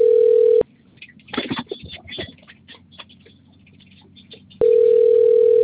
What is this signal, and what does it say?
Ringback tone (North America), 440 + 480 Hz.
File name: rg-1600-unknown- - - .27358